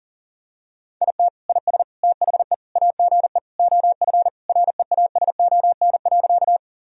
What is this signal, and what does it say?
Morse code, heard as a keyed tone, about 40 wpm, 700 Hz. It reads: IT IS THE AGE OF REASON.
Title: Sentences from Top 1000 Words - Review Part 3 - 40wpm